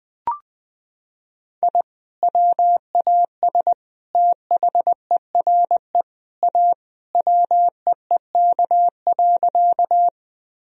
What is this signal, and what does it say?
Morse code, heard as a keyed tone, about 20 wpm, 700 Hz. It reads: I WAS THERE A WEEK.
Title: Sentences from Top 1000 Words - Review Part 2 - 20wpm